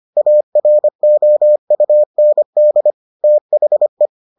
Morse code, heard as a keyed tone, about 25 wpm, 600 Hz. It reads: AROUND THE